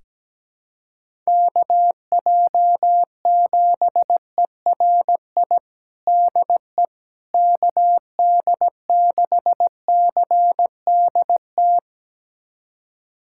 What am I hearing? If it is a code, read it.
Morse code, heard as a keyed tone, about 17 wpm, 700 Hz. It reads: KJ7ERI DE KD6CDT
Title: USA Callsign DE Callsign 17wpm